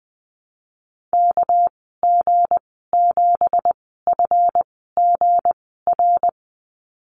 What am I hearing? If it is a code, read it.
Morse code, heard as a keyed tone, about 20 wpm, spaced as for 15 wpm, 700 Hz. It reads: KG7FGR
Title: 238 Lesson 38g - Letter q - Call Signs 20wpm